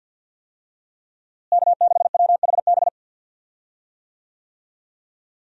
Morse code, heard as a keyed tone, about 52 wpm, 700 Hz. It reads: K6WHB
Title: USA Call Signs - 5 Characters - Repeated 2x - 50wpm